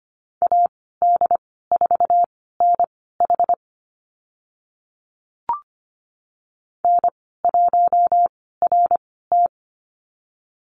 Morse code, heard as a keyed tone, about 25 wpm, spaced as for 16 wpm, 700 Hz. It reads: AD4NH N1RT
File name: Lesson 15g - Number 5 - Call Signs 25wpm